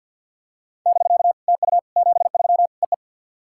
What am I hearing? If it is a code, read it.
Morse code, heard as a keyed tone, about 50 wpm, 700 Hz. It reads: <BK> TU 73 EE
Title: SOTA QSO Practice 50wpm